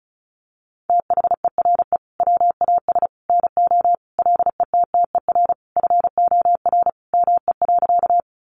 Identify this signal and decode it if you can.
Morse code, heard as a keyed tone, about 34 wpm, 700 Hz. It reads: THERE WAS NO LETTER FOR ME.
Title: Sentences from Top 300 Words 35wpm